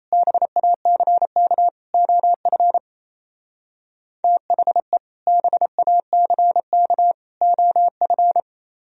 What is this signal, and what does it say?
Morse code, heard as a keyed tone, about 33 wpm, 700 Hz. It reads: BACK OF THE BACK OF
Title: Speed-Racing - Sets of 3 Words 28wpm